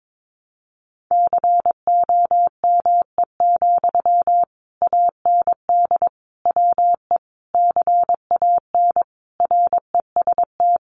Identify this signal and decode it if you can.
Morse code, heard as a keyed tone, about 22 wpm, 700 Hz. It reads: COME, AND WE CAN REST
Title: Sentences from Top 500 Words 22wpm